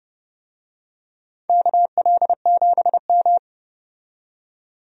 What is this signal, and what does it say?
Morse code, heard as a keyed tone, about 32 wpm, 700 Hz. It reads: KL7M
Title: DXCC-100 Call Signs - 4 Characters - 30wpm